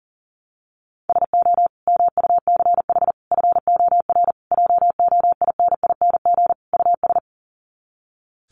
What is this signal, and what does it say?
Morse code, heard as a keyed tone, about 40 wpm, 700 Hz. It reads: SO MUCH FOR JOINING US
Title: Sets of 5 Words 40wpm